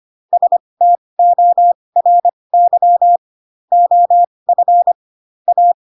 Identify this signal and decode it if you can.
Morse code, heard as a keyed tone, about 25 wpm, spaced as for 20 wpm, 700 Hz. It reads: STORY OF A
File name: Pinocchio / Pinocchio ch7 15 25 ik1hge